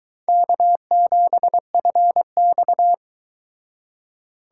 Morse code, heard as a keyed tone, about 22 wpm, 700 Hz. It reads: K7FX